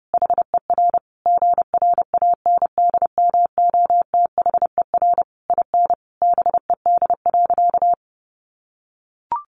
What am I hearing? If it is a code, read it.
Morse code, heard as a keyed tone, about 30 wpm, 700 Hz. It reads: HER GRANDMOTHER IN BED.
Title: Sentences from Top 1200 Words 30wpm